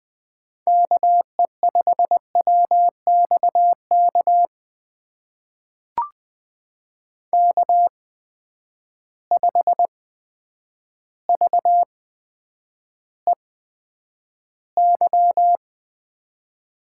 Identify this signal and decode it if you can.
Morse code, heard as a keyed tone, about 20 wpm, spaced as for 9 wpm, 700 Hz. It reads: KE5WXK K5VEY
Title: USA Call Signs - 8x Character Spacing - 20wpm